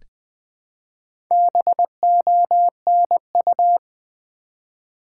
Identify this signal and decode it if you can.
Morse code, heard as a keyed tone, about 20 wpm, 700 Hz. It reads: BONU
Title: Build A Word - 1 Letter At A Time - 5-Letter Words - 20wpm